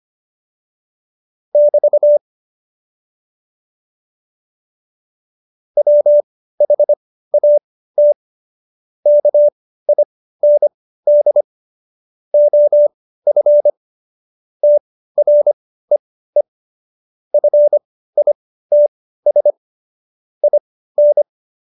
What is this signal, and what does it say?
Morse code, heard as a keyed tone, about 25 wpm, spaced as for 15 wpm, 590 Hz. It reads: <BT> WHAT KIND OF TREE FITS IN
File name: Jokes-25-char-15-eff